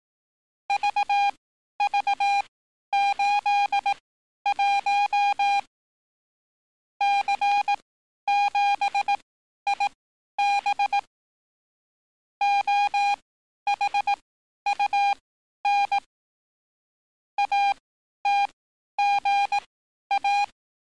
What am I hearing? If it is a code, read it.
Morse code, heard as a keyed tone, about 18 wpm, spaced as for 11 wpm, 810 Hz. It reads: VV81 C7IB OHUN ATGA
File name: training / NAVPERS 11049to11060 / NAVPERS 11051 RA RB